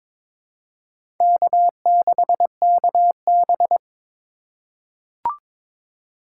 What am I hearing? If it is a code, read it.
Morse code, heard as a keyed tone, about 22 wpm, 700 Hz. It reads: K6KB
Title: USA Call Signs - 4 Characters - 2x Character Spacing - 22wpm